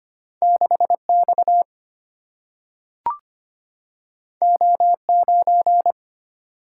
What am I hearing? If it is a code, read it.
Morse code, heard as a keyed tone, about 25 wpm, 700 Hz. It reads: 6X O9
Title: Single-Letter Single-Number 25wpm